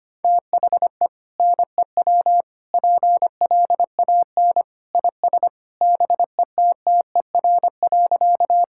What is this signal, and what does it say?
Morse code, heard as a keyed tone, about 25 wpm, 700 Hz. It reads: THE NEW PLAN IS BETTER.
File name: practice / 25wpm / Sentences with 2-Word Phrases - 25wpm